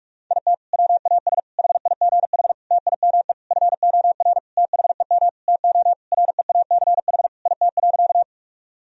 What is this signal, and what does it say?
Morse code, heard as a keyed tone, about 43 wpm, 700 Hz. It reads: IT WAS HIGH TIME FOR THEM TO REACH IT.
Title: Sentences from Top 1000 Words - Review Part 2 - 45wpm